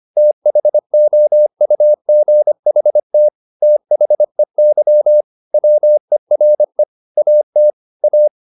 Morse code, heard as a keyed tone, about 25 wpm, 600 Hz. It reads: THOUGHT THEY WERE AT A
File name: the old man and the sea part 1 25wpm